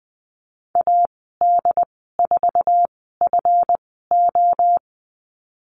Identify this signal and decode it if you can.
Morse code, heard as a keyed tone, about 20 wpm, spaced as for 15 wpm, 700 Hz. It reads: AD4FO
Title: 140 Lesson 23g - Letter f - Call Signs 20wpm